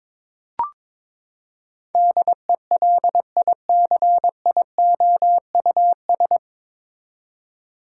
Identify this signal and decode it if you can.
Morse code, heard as a keyed tone, about 21 wpm, 700 Hz. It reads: DELICIOUS
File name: practice / 22wpm / Nine Letter Words - 22wpm